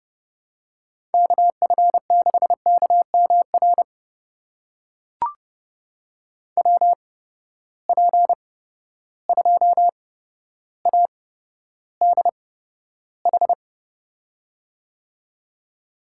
Morse code, heard as a keyed tone, about 31 wpm, spaced as for 13 wpm, 700 Hz. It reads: KF6KMR WP2ADH
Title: USA Call Signs - 6 Characters - 8x Character Spacing - 30wpm